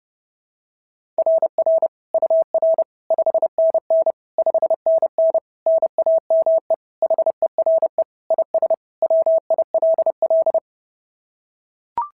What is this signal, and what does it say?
Morse code, heard as a keyed tone, about 30 wpm, 690 Hz. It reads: RR UR 5NN 5NN NAME HERE IS WILL
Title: SOTA QSO Practice 30wpm